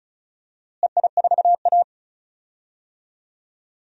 Morse code, heard as a keyed tone, about 36 wpm, 700 Hz. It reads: EI4A
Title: DXCC-100 Call Signs - 4 Characters - Repeated 3x - 35wpm